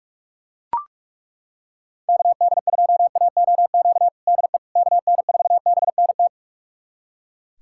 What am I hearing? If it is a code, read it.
Morse code, heard as a keyed tone, about 42 wpm, 700 Hz. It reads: KD2AOQ DE KN4BNT
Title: USA Callsign DE Callsign - 6 Characters - 45wpm